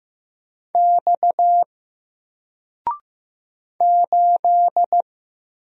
Morse code, heard as a keyed tone, about 15 wpm, 700 Hz. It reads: X 8
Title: Single Letter-Number 15wpm